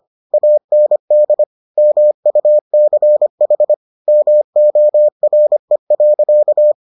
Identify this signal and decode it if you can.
Morse code, heard as a keyed tone, about 25 wpm, 600 Hz. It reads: AND MUCH MORE.